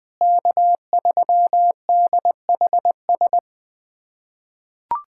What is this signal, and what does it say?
Morse code, heard as a keyed tone, about 20 wpm, 700 Hz. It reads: K3DHS
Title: USA Callsign DE Callsign - 5 Characters - 20wpm